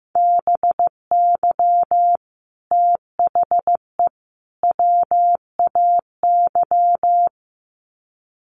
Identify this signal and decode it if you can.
Morse code, heard as a keyed tone, about 15 wpm, 700 Hz. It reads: BY THE WAY
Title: Sets of 3 Words 15wpm